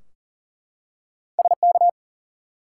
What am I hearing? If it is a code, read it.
Morse code, heard as a keyed tone, about 38 wpm, 700 Hz. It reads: SK
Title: US States and Canadian Provinces 40wpm